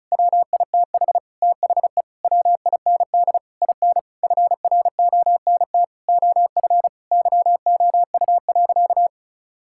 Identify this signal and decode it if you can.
Morse code, heard as a keyed tone, about 36 wpm, 700 Hz. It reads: WITH THE WIND IN FRONT OF YOU.
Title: Sentences from Top 1000 Words - Review Part 6 - 35wpm